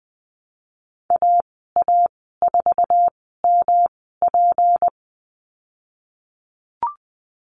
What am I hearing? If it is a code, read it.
Morse code, heard as a keyed tone, about 20 wpm, spaced as for 15 wpm, 700 Hz. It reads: AA4MP
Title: 154 Lesson 25g - Letter p - Call Signs 20wpm